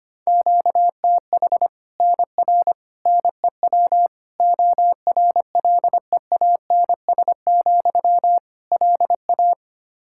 Morse code, heard as a keyed tone, about 25 wpm, 700 Hz. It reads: QTH NR NEW ORLEANS, LA
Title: Practice QSO - Exchange 1 - Beginning - 25wpm